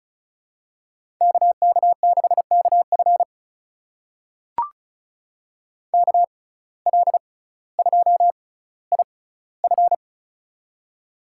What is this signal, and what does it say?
Morse code, heard as a keyed tone, about 36 wpm, spaced as for 18 wpm, 700 Hz. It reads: KK6KF KL2IF
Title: USA Call Signs - 5 Characters - 6x Character Spacing - 35wpm